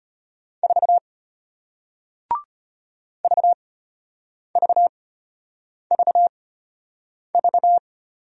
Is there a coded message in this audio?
Yes: Morse code from 0.6 to 7.8 s, heard as a keyed tone, about 34 wpm, spaced as for 8 wpm, 700 Hz. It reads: 4 VVVV